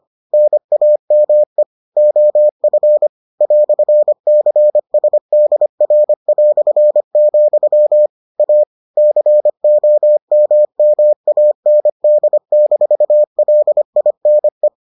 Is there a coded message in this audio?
Yes: Morse code from 0.3 to 14.7 s, heard as a keyed tone, about 25 wpm, 600 Hz. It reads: NAME OF "CSDR", A COMMAND-LINE